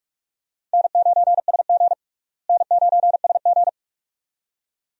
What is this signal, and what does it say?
Morse code, heard as a keyed tone, about 43 wpm, 700 Hz. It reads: N9SG N9SG
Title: USA Call Signs - 4 Characters - Repeated 2x - 45wpm